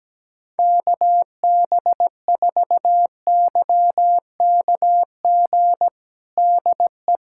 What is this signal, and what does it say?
Morse code, heard as a keyed tone, about 17 wpm, 700 Hz. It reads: KB4YKG DE
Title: USA Callsign DE Callsign 17wpm